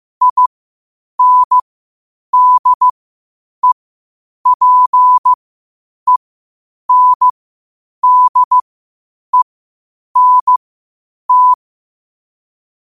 Morse code, heard as a keyed tone, about 15 wpm, spaced as for 8 wpm, 1000 Hz. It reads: INDEPENDENT
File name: qotdfiles / 8wpm